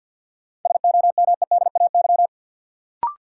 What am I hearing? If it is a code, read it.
Morse code, heard as a keyed tone, about 52 wpm, 700 Hz. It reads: SOMEDAY
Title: Sentences from Top 1600 Words - New Word Drill Alt - 50wpm